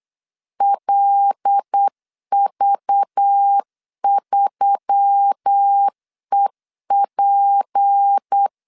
Morse code, heard as a keyed tone, about 8 wpm, 800 Hz. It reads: LV3EP
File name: LEZIONI / 6 PKX0 / 6 PK 8wpm 3003